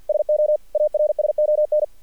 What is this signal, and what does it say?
Morse code, heard as a keyed tone, about 48 wpm, 600 Hz. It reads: SO ARSON
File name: archived / CRSnet